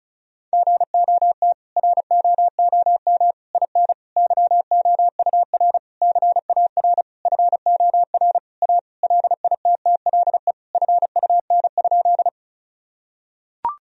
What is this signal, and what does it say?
Morse code, heard as a keyed tone, about 36 wpm, 700 Hz. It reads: GOT ROOM IN YOUR CAR FOR A LITTLE FUN?
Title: Sentences from Top 1000 Words - Review Part 1 - 35wpm